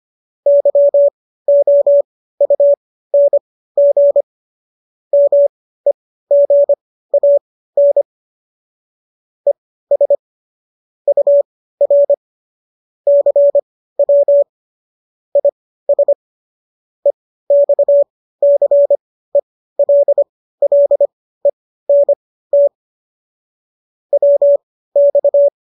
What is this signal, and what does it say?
Morse code, heard as a keyed tone, about 25 wpm, spaced as for 15 wpm, 570 Hz. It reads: YOUNG MEGAN ES UR CW IS EXCELLENT WX